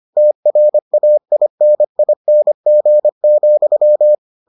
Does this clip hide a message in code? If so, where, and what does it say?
Yes: Morse code from 0.2 to 4.1 s, heard as a keyed tone, about 25 wpm, 600 Hz. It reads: TRAINING,